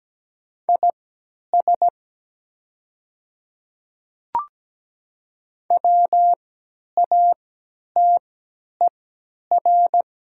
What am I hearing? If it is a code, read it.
Morse code, heard as a keyed tone, about 17 wpm, spaced as for 10 wpm, 700 Hz. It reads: IS WATER
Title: 3x Character Spacing - Top 100 Words - 17wpm